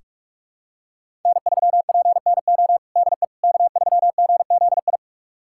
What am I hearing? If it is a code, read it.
Morse code, heard as a keyed tone, about 46 wpm, 700 Hz. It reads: N3PNO DE K3GZI
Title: USA Callsign DE Callsign - 5 Characters - 45wpm